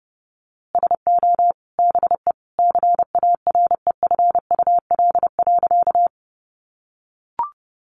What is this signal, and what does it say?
Morse code, heard as a keyed tone, about 30 wpm, 700 Hz. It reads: SO BE CAREFUL.
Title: A Sound Of Thunder 30wpm